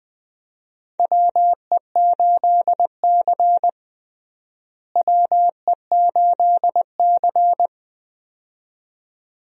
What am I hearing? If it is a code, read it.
Morse code, heard as a keyed tone, about 20 wpm, 700 Hz. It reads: WE8C WE8C
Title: USA Call Signs - 4 Characters - Repeated 2x - 20wpm